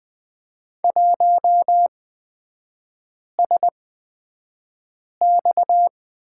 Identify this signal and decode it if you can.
Morse code, heard as a keyed tone, about 20 wpm, spaced as for 5 wpm, 700 Hz. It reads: 1SX
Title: 243 Lesson 39e - Letter z - Three Characters 20wpm